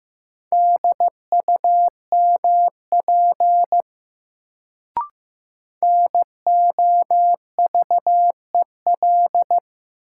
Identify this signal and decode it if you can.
Morse code, heard as a keyed tone, about 15 wpm, 700 Hz. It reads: DUMP NOVEL
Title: Sentences from Top 1300 Words - New Word Drill Alt - 15wpm